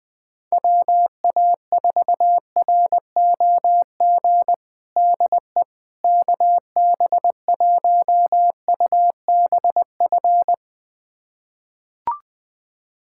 Morse code, heard as a keyed tone, about 21 wpm, 700 Hz. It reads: WA4ROG DE KB1UBF
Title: USA Callsign DE Callsign - 6 Characters - 20wpm